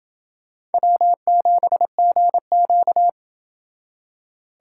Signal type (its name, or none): Morse code